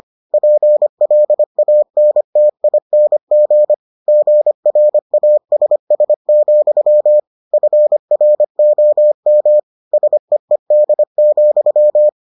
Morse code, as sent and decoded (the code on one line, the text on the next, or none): .--. .-.. .- -. - .. -. --. / --. .-. .- ... ... --..-- / ..-. .-. --- -- / ... . -.. --..--
PLANTING GRASS, FROM SEED,